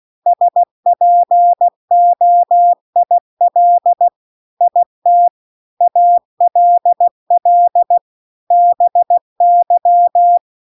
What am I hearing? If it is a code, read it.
Morse code, heard as a keyed tone, about 16 wpm, 700 Hz. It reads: SPOIL IT ALL BY